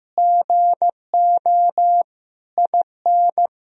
Morse code, heard as a keyed tone, about 15 wpm, 700 Hz. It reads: GO IN
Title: Sentences from Top 1000 Words - Review Part 4 - 15wpm